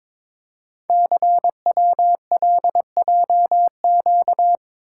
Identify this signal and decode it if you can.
Morse code, heard as a keyed tone, about 22 wpm, 700 Hz. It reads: CWLJQ
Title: Code Groups - 5 Letters - 22wpm